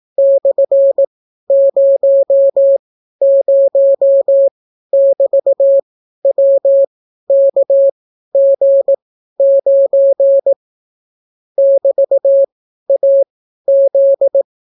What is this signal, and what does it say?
Morse code, heard as a keyed tone, about 18 wpm, spaced as for 12 wpm, 550 Hz. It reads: /00<BT>WKG9 <BT>AZ